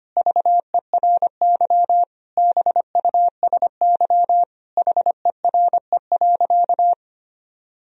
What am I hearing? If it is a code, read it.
Morse code, heard as a keyed tone, about 25 wpm, 700 Hz. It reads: VERY BUSY HERE.
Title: Sentences from Top 1000 Words - Review Part 3 - 25wpm